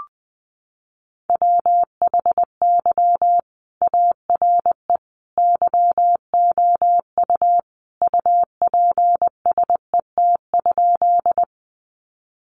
Morse code, heard as a keyed tone, about 20 wpm, 700 Hz. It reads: WHY ARE YOU UPSET?
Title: Sentences from Top 800 Words 20wpm